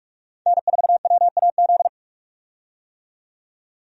Morse code, heard as a keyed tone, about 46 wpm, 700 Hz. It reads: N4WAZ
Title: USA Call Signs - 5 Characters - 45wpm